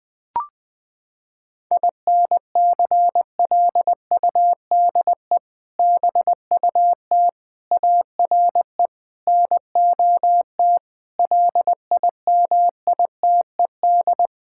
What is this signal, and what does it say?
Morse code, heard as a keyed tone, about 20 wpm, 700 Hz. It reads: INCLUDE BUT ARE NOT LIMITED
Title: Sets of 6 Words - 20wpm